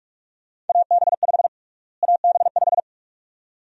Morse code, heard as a keyed tone, about 46 wpm, 700 Hz. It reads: AB5 AB5
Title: USA Call Sign Prefixes - Repeated 2x - 45wpm